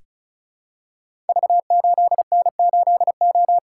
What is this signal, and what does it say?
Morse code, heard as a keyed tone, about 36 wpm, 690 Hz. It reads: V8N8O
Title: Code Groups - 5 Letter:Numbers - 35wpm